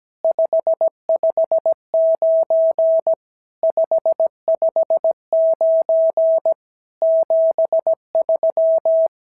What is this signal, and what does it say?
Morse code, heard as a keyed tone, about 16 wpm, 650 Hz. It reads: 559 559 73
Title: SOTA QSO Practice 17wpm